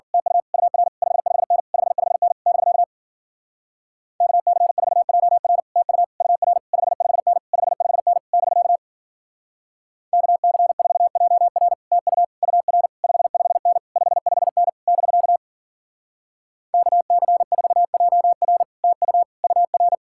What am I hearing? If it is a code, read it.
Morse code, heard as a keyed tone, about 50 wpm, 700 Hz. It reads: TU UR 55N 55N <BK> KC4JR TU UR 55N 55N <BK> KC4JR TU UR 55N 55N <BK> KC4JR TU UR